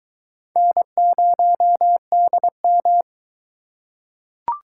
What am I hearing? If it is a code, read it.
Morse code, heard as a keyed tone, about 22 wpm, 700 Hz. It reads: N0DM